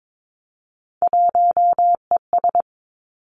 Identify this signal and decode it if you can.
Morse code, heard as a keyed tone, about 22 wpm, 700 Hz. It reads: 1ES
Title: Three Challenging Letter-Number Combinations 22wpm